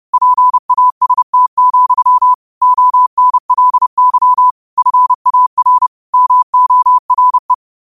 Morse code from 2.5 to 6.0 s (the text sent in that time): ONLY FAR